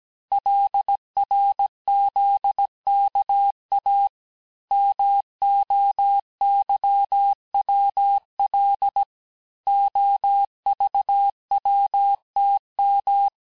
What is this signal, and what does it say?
Morse code, heard as a keyed tone, about 17 wpm, 800 Hz. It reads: LRZKA MOYWL OVWTM
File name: corrige-cours-1581-texte-chiffre-vitesse-16-mpm